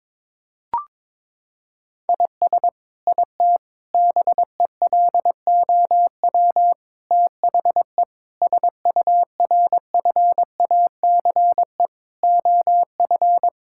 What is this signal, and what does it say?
Morse code, heard as a keyed tone, about 22 wpm, 700 Hz. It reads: IS IT BELOW THE SURFACE OF
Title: Sentences from Top 1000 Words - Review Part 5 - 22wpm